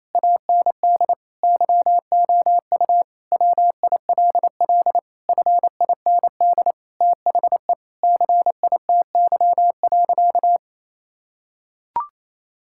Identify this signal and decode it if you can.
Morse code, heard as a keyed tone, about 28 wpm, 700 Hz. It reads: AND YOU WILL FIND THE CITY.
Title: Sentences from Top 1000 Words - Review Part 6 - 28wpm